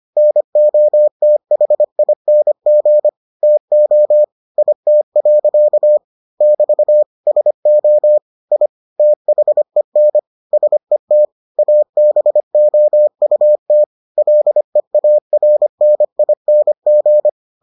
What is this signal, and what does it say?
Morse code, heard as a keyed tone, about 25 wpm, 600 Hz. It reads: NOTHING TO IT. <BT> SO I THEN SET ABOUT LEARNING